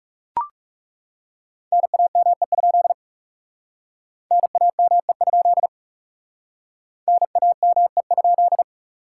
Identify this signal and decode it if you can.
Morse code, heard as a keyed tone, about 37 wpm, 700 Hz. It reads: NAME? NAME? NAME?
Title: Speed-Racing Common QSO Phrases Encore - 30wpm